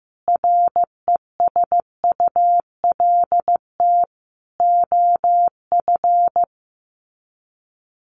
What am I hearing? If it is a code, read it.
Morse code, heard as a keyed tone, about 15 wpm, 700 Hz. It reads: RESULT OF
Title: Sets of 3 Words 15wpm